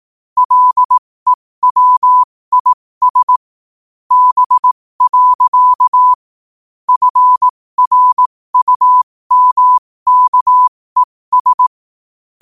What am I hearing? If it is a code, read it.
Morse code, heard as a keyed tone, about 18 wpm, spaced as for 15 wpm, 1000 Hz. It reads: LEWIS B. FRUMKES
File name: qotdfiles / 15wpm